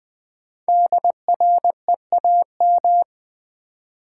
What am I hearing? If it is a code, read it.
Morse code, heard as a keyed tone, about 20 wpm, 700 Hz. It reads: DREAM
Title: Sentences from Top 1500 Words - New Word Drill - 20wpm